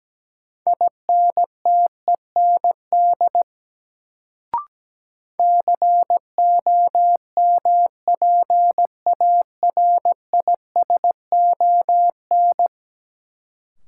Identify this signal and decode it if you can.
Morse code, heard as a keyed tone, about 17 wpm, 700 Hz. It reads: INTEND COMPARISON
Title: Sentences from Top 1600 Words - New Word Drill Alt - 17wpm